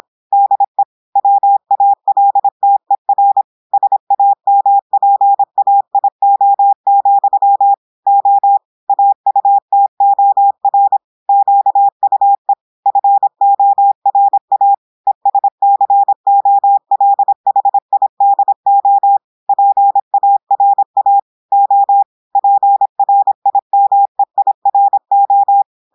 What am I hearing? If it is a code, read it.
Morse code, heard as a keyed tone, about 26 wpm, 800 Hz. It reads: DE WALTER SAMPAIO, O AUTOR QUE FORA ESCOLHIDO PARA O PRIMEIRO
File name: audios / portugais / portugais texte en clair 1 vitesse 26 mots minutes